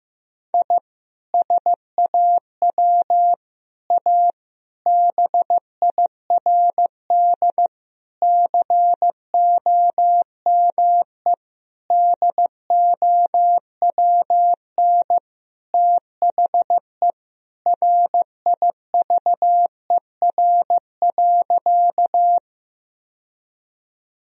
Morse code, heard as a keyed tone, about 15 wpm, 700 Hz. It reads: I SAW A BIRD COME DOWN THE RIVER.
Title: Sentences from Top 1000 Words - Review Part 2 - 15wpm